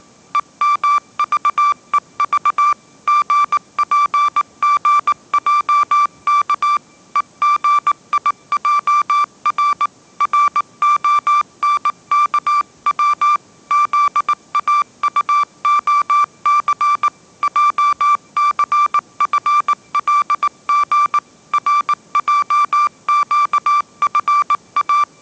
Morse code, heard as a keyed tone, about 19 wpm, 1170 Hz. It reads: EMVEV GPGJK EGIJR RONKW ZAUOC JCFLG RJQFA
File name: mrx083 20wpm